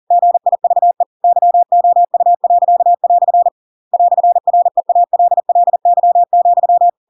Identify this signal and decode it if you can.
Morse code, heard as a keyed tone, about 40 wpm, 700 Hz. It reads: GIVE YOU." "REALLY,
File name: Pinocchio / Pinocchio ch7 40 40 ik1hge